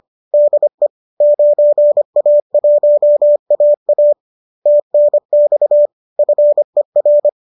Morse code, heard as a keyed tone, about 25 wpm, 590 Hz. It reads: DE 9A1AA TNX FER